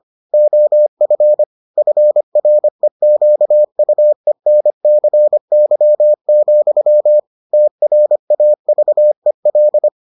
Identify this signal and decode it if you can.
Morse code, heard as a keyed tone, about 25 wpm, 600 Hz. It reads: OF FREQUENCY, TRAVEL